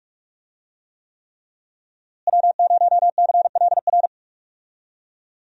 Morse code, heard as a keyed tone, about 42 wpm, 700 Hz. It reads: W0CLR
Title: USA Call Signs - 5 Characters - Repeated 3x - 45wpm